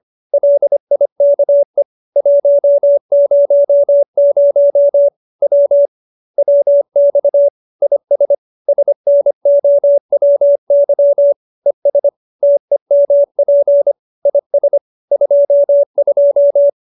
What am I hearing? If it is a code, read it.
Morse code, heard as a keyed tone, about 25 wpm, 570 Hz. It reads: LIKE 100 W WX IS SNOWY ES TEMP IS 22